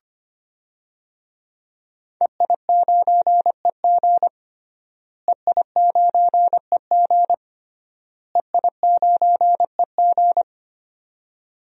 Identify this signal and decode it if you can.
Morse code, heard as a keyed tone, about 25 wpm, 700 Hz. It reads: EI9EG EI9EG EI9EG
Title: DXCC-100 Call Signs - 5 Characters - Repeated 3x - 25wpm